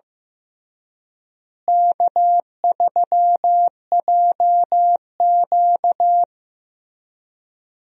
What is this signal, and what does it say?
Morse code, heard as a keyed tone, about 15 wpm, 700 Hz. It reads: K3JQ
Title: USA Call Signs - 4 Characters - Repeated 3x - 15wpm